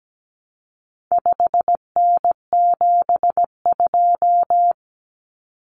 Morse code, heard as a keyed tone, about 17 wpm, 700 Hz. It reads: 5N72
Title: Common Cut Numbers - 4-Digit 17wpm